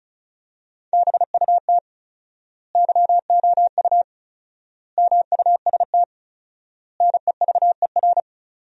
Morse code, heard as a keyed tone, about 36 wpm, 700 Hz. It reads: BUT YOU MUST NEVER
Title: 4x Word-Spacing - Sentences from Top 300 Words - 35wpm